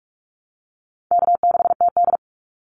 Morse code, heard as a keyed tone, about 44 wpm, 700 Hz. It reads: K6TD
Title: Call Signs Repeated 3x - 45wpm